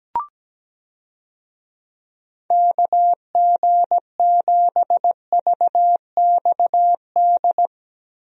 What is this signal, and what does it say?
Morse code, heard as a keyed tone, about 17 wpm, 700 Hz. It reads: KG7VXD
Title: USA Call Signs - 6 Characters - Repeated 3x - 17wpm